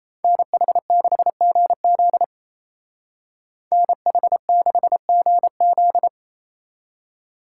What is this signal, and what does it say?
Morse code, heard as a keyed tone, about 36 wpm, 700 Hz. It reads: NH6GZ NH6GZ